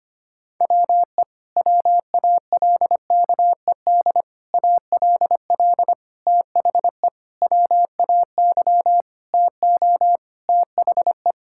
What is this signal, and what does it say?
Morse code, heard as a keyed tone, about 25 wpm, 700 Hz. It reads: WE WALKED ALL THE WAY TO THE
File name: practice / 25wpm / Sentences with 4-Word Phrases - 25wpm